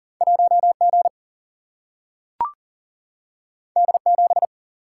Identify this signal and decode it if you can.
Morse code, heard as a keyed tone, about 40 wpm, 700 Hz. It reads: 1G D7
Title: Single-Letter Single-Number 40wpm